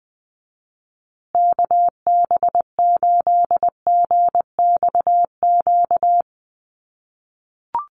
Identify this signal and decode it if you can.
Morse code, heard as a keyed tone, about 20 wpm, 700 Hz. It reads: KB8GXQ
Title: USA Call Signs Repeated 2x 20wpm